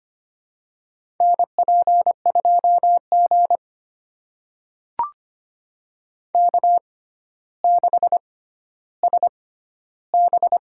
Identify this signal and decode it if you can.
Morse code, heard as a keyed tone, about 25 wpm, spaced as for 13 wpm, 700 Hz. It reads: NP2G K6SB
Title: USA Call Signs - 4 Characters - 6x Character Spacing - 25wpm